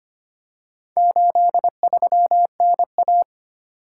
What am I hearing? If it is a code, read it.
Morse code, heard as a keyed tone, about 25 wpm, 700 Hz. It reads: 83NA
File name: practice / 25wpm / Four Letter-Number Combinations - 25wpm